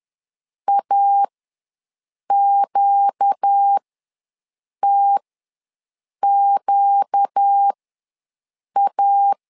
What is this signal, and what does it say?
Morse code, heard as a keyed tone, about 10 wpm, spaced as for 6 wpm, 800 Hz. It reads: AQTQA
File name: LEZIONI / 4 Q ws 10wpm 2306